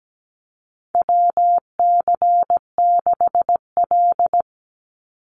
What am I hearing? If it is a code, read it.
Morse code, heard as a keyed tone, about 17 wpm, 700 Hz. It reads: WC6L
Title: USA Call Signs 17wpm